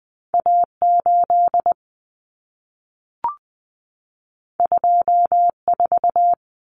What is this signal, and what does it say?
Morse code, heard as a keyed tone, about 20 wpm, 700 Hz. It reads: A8 24
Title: Common Cut Numbers - 2-Digit 20wpm